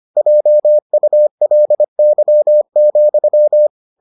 Morse code, heard as a keyed tone, about 25 wpm, 600 Hz. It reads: JULY,